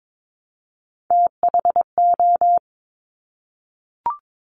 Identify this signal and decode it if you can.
Morse code, heard as a keyed tone, about 22 wpm, 700 Hz. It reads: THO
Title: Top Three Letter Combinations in Words 22wpm